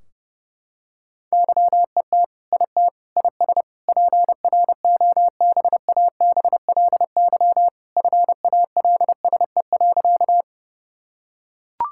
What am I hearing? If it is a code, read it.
Morse code, heard as a keyed tone, about 30 wpm, 700 Hz. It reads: YET IT IS PROBABLY FALSE.
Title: Sentences from Top 1600 Words 30wpm